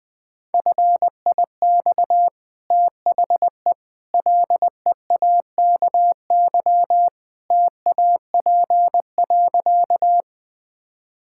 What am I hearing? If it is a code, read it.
Morse code, heard as a keyed tone, about 20 wpm, 700 Hz. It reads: FIX THE LEAKY TAP.
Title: Easy Sentences with Hints - 20wpm